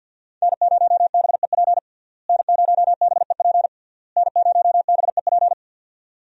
Morse code, heard as a keyed tone, about 47 wpm, 700 Hz. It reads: N0BEP N0BEP N0BEP
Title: USA Call Signs - 5 Characters - Repeated 3x - 50wpm